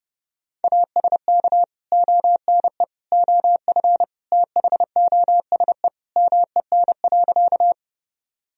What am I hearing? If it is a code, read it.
Morse code, heard as a keyed tone, about 30 wpm, 700 Hz. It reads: ASK ONE OF THOSE MEN.
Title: Sentences from Top 1000 Words - Review Part 1 - 30wpm